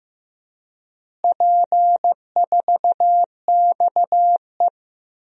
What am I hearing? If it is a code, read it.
Morse code, heard as a keyed tone, about 15 wpm, 700 Hz. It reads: P4XE
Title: DXCC-100 Call Signs - 4 Characters - Repeated 3x - 15wpm